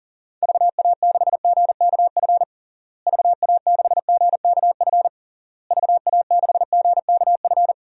Morse code, heard as a keyed tone, about 37 wpm, 700 Hz. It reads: VA6GKF VA6GKF VA6GKF
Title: DXCC-100 Call Signs - 6 Characters - Repeated 3x - 40wpm